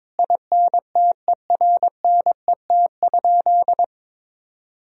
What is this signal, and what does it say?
Morse code, heard as a keyed tone, about 22 wpm, 700 Hz. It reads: INTERNET?
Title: Sentences from Top 1000 Words - Review Part 4 - 22wpm